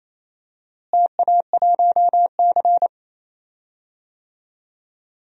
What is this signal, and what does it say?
Morse code, heard as a keyed tone, about 28 wpm, 700 Hz. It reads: TA1C
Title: DXCC-100 Call Signs - 4 Characters - Repeated 3x - 28wpm